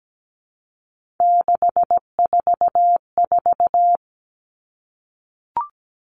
Morse code, heard as a keyed tone, about 17 wpm, 700 Hz. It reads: 644
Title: Numbers - 3-Digit 17wpm